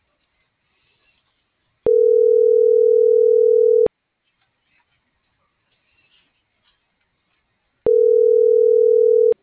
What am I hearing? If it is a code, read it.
Ringback tone (North America), 440 + 480 Hz.